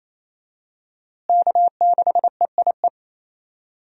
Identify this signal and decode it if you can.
Morse code, heard as a keyed tone, about 28 wpm, 700 Hz. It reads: K6EIE
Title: USA Call Signs - 5 Characters - Repeated 3x - 28wpm